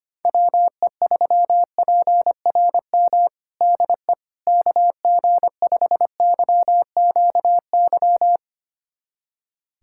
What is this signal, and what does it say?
Morse code, heard as a keyed tone, about 25 wpm, 700 Hz. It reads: WE3PRM DE KG5YQY
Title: USA Callsign DE Callsign - 6 Characters - 25wpm